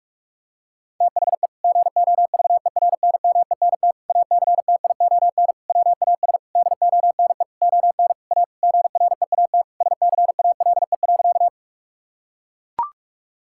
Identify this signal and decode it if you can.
Morse code, heard as a keyed tone, about 41 wpm, 700 Hz. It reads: THE GOVERNMENT ACTION WAS DONE ON A GREAT SCALE.
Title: Sentences from Top 1000 Words - Review Part 5 - 45wpm